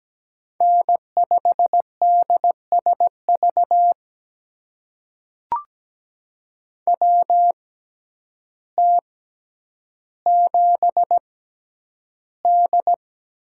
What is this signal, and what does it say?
Morse code, heard as a keyed tone, about 17 wpm, spaced as for 9 wpm, 700 Hz. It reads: N5DSV WT7D
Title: USA Call Signs - 6x Character Spacing - 17wpm